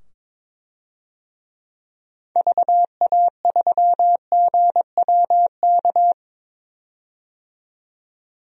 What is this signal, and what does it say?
Morse code, heard as a keyed tone, about 22 wpm, 700 Hz. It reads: VA3GWK
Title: DXCC-100 Call Signs - 6 Characters - 22wpm